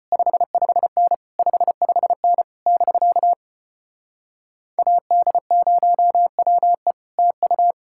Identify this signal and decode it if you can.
Morse code, heard as a keyed tone, about 33 wpm, 700 Hz. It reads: 55N 55N <BK> AD0WE TU